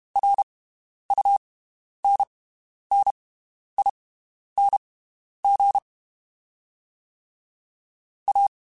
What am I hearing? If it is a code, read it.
Morse code, heard as a keyed tone, about 32 wpm, spaced as for 11 wpm, 800 Hz. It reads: RUNNING A
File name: (spaced) / QSOs Spaced-mp3s / WD0FBT 1 @30 sp